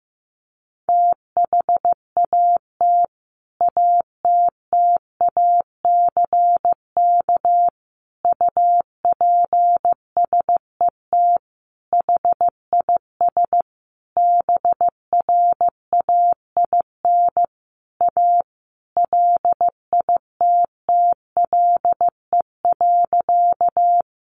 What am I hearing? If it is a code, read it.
Morse code, heard as a keyed tone, about 15 wpm, 700 Hz. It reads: THAT ATTACK UPSET HIS BRAIN A LITTLE.
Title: Sentences from Top 1500 Words 15wpm